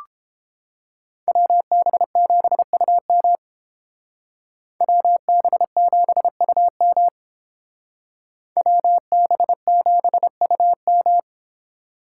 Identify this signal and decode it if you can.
Morse code, heard as a keyed tone, about 30 wpm, 700 Hz. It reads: WB7UM WB7UM WB7UM